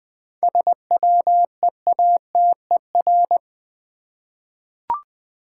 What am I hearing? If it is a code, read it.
Morse code, heard as a keyed tone, about 20 wpm, 700 Hz. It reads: SWEATER
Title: 7 Letter Words - First 3 Letters Given - 20wpm